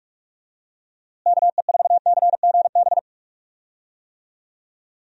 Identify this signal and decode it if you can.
Morse code, heard as a keyed tone, about 46 wpm, 700 Hz. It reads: KE4CGB
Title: USA Call Signs - 6 Characters - Repeated 3x - 45wpm